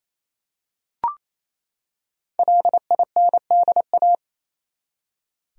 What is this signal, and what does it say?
Morse code, heard as a keyed tone, about 28 wpm, 700 Hz. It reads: LINDA
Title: USA Common Names - 28wpm